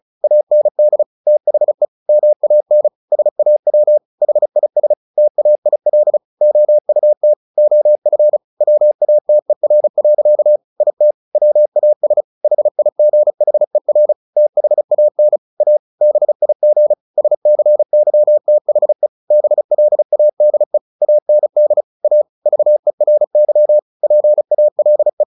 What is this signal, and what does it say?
Morse code, heard as a keyed tone, about 34 wpm, 600 Hz. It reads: AND THE MAN SAW HIS TAIL OUT OF WATER. IT WAS HIGHER THAN A BIG SCYTHE BLADE AND A VERY PALE
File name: the old man and the sea 2 35wpm